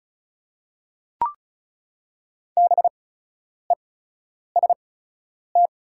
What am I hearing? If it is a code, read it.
Morse code, heard as a keyed tone, about 36 wpm, spaced as for 10 wpm, 690 Hz. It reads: BEST
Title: 8x Character Spacing - Top 500 Words - 35wpm